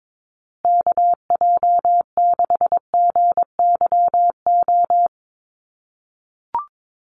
Morse code, heard as a keyed tone, about 22 wpm, 700 Hz. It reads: KJ6GYO
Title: USA Call Signs Repeated 2x 22wpm